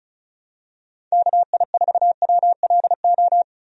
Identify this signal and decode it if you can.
Morse code, heard as a keyed tone, about 34 wpm, 700 Hz. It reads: KI4WLO
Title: USA Call Signs - 6 Characters - Repeated 2x - 35wpm